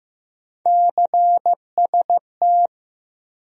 Morse code, heard as a keyed tone, about 15 wpm, 700 Hz. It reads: CST
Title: USA Call Sign Postfixes - 3 Letters 15wpm